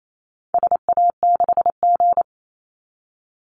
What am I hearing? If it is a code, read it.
Morse code, heard as a keyed tone, about 28 wpm, 700 Hz. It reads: SA6G
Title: Call Signs - 28wpm